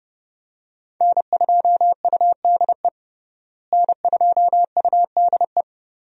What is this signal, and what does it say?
Morse code, heard as a keyed tone, about 30 wpm, 700 Hz. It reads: N2UDE N2UDE
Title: USA Call Signs - 5 Characters - Repeated 3x - 30wpm